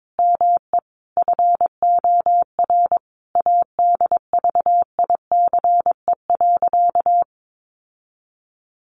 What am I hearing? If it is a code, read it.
Morse code, heard as a keyed tone, about 22 wpm, 700 Hz. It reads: ME FOR ADVICE.
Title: Sentences from Top 1100 Words 22wpm